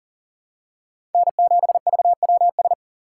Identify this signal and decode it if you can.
Morse code, heard as a keyed tone, about 40 wpm, 700 Hz. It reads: N7VWS
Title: USA Call Signs - 5 Characters - 4x Character Spacing - 40wpm